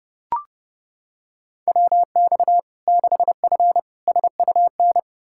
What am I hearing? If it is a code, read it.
Morse code, heard as a keyed tone, about 32 wpm, 700 Hz. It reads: WX 6F SUN
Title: Practice QSO - Exchange 2 - Info - 30wpm